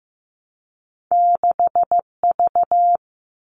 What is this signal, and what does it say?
Morse code, heard as a keyed tone, about 15 wpm, 700 Hz. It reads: 6V
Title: All Cut Numbers - 2-Digit 15wpm